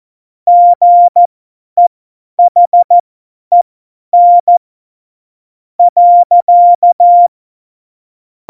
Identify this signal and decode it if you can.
Morse code, heard as a keyed tone, about 14 wpm, spaced as for 9 wpm, 700 Hz. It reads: GEHEN.
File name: Morsekurs / Witze / Witz 22. Mai 2018, 50 F70